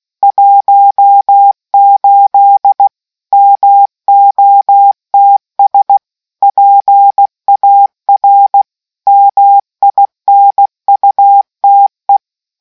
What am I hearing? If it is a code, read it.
Morse code, heard as a keyed tone, about 16 wpm, 800 Hz. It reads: 18 MOTS PAR MINUTE